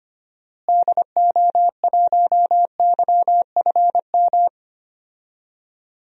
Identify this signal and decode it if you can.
Morse code, heard as a keyed tone, about 25 wpm, 700 Hz. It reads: DO1YFM
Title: DXCC-100 Call Signs - 6 Characters - Repeated 2x - 25wpm